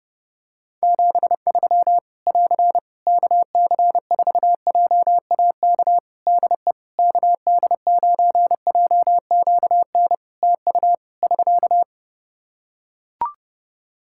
Morse code, heard as a keyed tone, about 30 wpm, 700 Hz. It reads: 73 <AR> KC4JAK DE KD9JQN TU <SK>
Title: Practice QSO - Exchange 4 - Ending - 30wpm